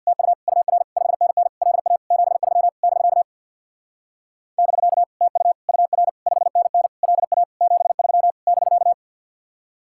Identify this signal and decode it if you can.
Morse code, heard as a keyed tone, about 64 wpm, 700 Hz. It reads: TU UR 5NN LA 73 <BK> <BK> TU UR 5NN LA 73 <BK>